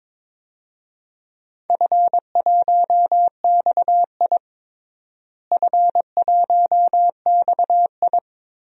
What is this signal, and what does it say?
Morse code, heard as a keyed tone, about 22 wpm, 700 Hz. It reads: F1XI F1XI
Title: DXCC-100 Call Signs - 4 Characters - Repeated 3x - 22wpm